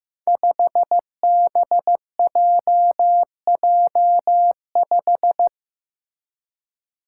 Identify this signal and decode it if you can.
Morse code, heard as a keyed tone, about 15 wpm, 700 Hz. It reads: 5BJJ5
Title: Five Challenging Letter-Number Combinations 15wpm